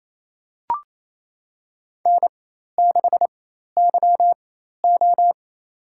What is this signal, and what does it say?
Morse code, heard as a keyed tone, about 28 wpm, spaced as for 13 wpm, 700 Hz. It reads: N6YO
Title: USA Call Signs - 4 Characters - 4x Character Spacing - 28wpm